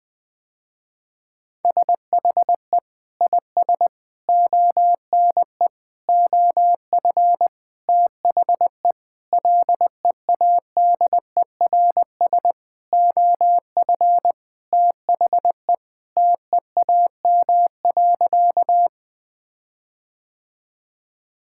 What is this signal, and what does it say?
Morse code, heard as a keyed tone, about 21 wpm, 700 Hz. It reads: SHE IS ONE OF THE LEADERS OF THE TEAM.
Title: Sentences with 6-Word Phrases - Spoken Beginning - 20wpm